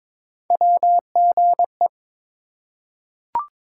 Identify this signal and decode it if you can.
Morse code, heard as a keyed tone, about 22 wpm, 700 Hz. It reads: WGE
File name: practice / 22wpm / Three Letter-Number Combinations - 22wpm